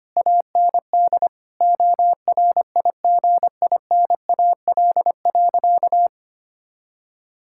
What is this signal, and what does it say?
Morse code, heard as a keyed tone, about 25 wpm, 700 Hz. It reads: AND ORIGINAL.
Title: Sentences from Top 1000 Words - Review Part 6 - 25wpm